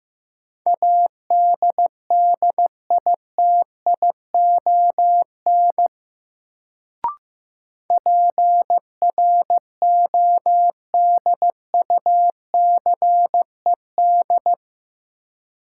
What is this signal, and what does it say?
Morse code, heard as a keyed tone, about 15 wpm, 700 Hz. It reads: ADDITION PRODUCED
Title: Sentences from Top 1000 Words - Review Part 6 - New Word Drill - 15wpm